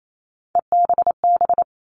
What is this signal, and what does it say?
Morse code, heard as a keyed tone, about 28 wpm, 700 Hz. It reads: EBB
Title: All Cut Numbers - 3-Digit 28wpm